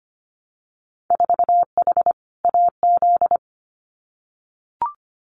Morse code, heard as a keyed tone, about 25 wpm, 700 Hz. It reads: 4H AZ
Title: Winter Field Day Exchanges Encore 25wpm